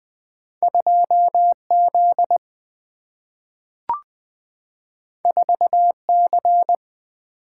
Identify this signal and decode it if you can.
Morse code, heard as a keyed tone, about 20 wpm, 700 Hz. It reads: 2Z 4C
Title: Single-Letter Single-Number 20wpm